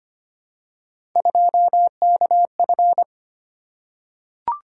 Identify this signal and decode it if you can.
Morse code, heard as a keyed tone, about 25 wpm, 700 Hz. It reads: 2KF
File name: Three Letter-Number Combinations - 25wpm